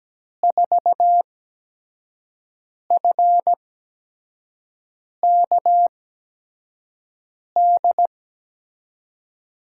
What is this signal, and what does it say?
Morse code, heard as a keyed tone, about 17 wpm, spaced as for 5 wpm, 700 Hz. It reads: 4FKD
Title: USA Call Signs - 6 Characters - 8x Character Spacing - 17wpm